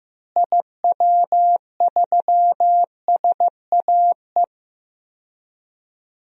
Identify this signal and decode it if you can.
Morse code, heard as a keyed tone, about 15 wpm, 700 Hz. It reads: IW3SAE
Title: DXCC-100 Call Signs - 6 Characters - Repeated 3x - 15wpm